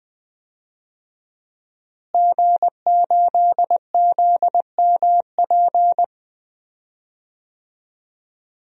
Morse code, heard as a keyed tone, about 20 wpm, 700 Hz. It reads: G8ZMP
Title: DXCC-100 Call Signs - 5 Characters - Repeated 3x - 20wpm